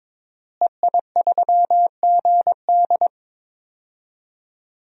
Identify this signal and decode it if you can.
Morse code, heard as a keyed tone, about 21 wpm, 700 Hz. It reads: EI3GD
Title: DXCC-100 Call Signs - 5 Characters - Repeated 2x - 22wpm